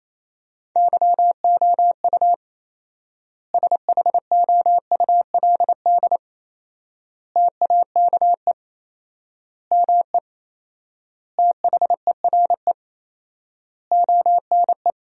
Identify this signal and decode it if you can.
Morse code, heard as a keyed tone, about 28 wpm, 700 Hz. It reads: YOU SHOULD TAKE ME THERE ONE
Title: 4x Word-Spacing - Sentences from Top 300 Words - 28wpm